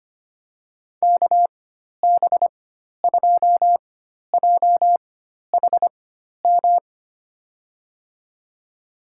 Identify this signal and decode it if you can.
Morse code, heard as a keyed tone, about 25 wpm, spaced as for 12 wpm, 700 Hz. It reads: KB2JHM
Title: USA Call Signs - 6 Characters - 4x Character Spacing - 25wpm